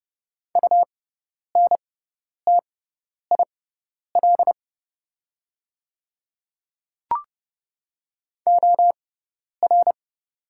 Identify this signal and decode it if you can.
Morse code, heard as a keyed tone, about 30 wpm, spaced as for 10 wpm, 700 Hz. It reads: UNTIL OR
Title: 6x Character Spacing - Top 500 Words - 30wpm